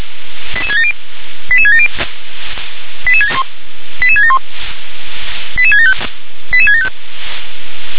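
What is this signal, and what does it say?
Five-tone selective call (ZVEI), heard as quick runs of five tones, tones of about 70 ms. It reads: 88670 88670 88671 88675 88675
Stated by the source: Five-tone ZVEI selective call received on SDR.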